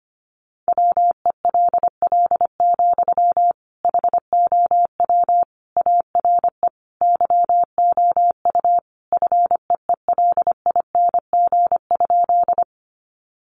Morse code, heard as a keyed tone, about 25 wpm, 700 Hz. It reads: WELL, HOW ARE YOU FEELING?
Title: Sentences from Top 1200 Words 25wpm